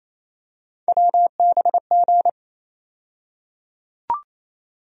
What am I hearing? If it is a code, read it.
Morse code, heard as a keyed tone, about 27 wpm, 700 Hz. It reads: WBG
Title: USA Call Sign Postfixes - 3 Letters 28wpm